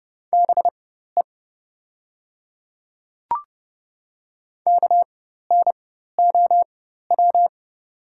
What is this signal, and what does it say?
Morse code, heard as a keyed tone, about 30 wpm, spaced as for 14 wpm, 700 Hz. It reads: BE KNOW
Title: 4x Character Spacing - Top 100 Words - 30wpm